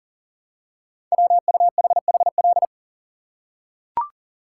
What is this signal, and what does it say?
Morse code, heard as a keyed tone, about 44 wpm, 700 Hz. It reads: WUHHL
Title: Code Groups - 5 Letters - 40wpm